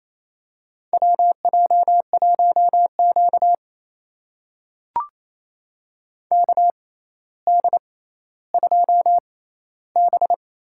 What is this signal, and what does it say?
Morse code, heard as a keyed tone, about 28 wpm, spaced as for 14 wpm, 700 Hz. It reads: WJ1Q KD2B
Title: USA Call Signs - 6x Character Spacing - 28wpm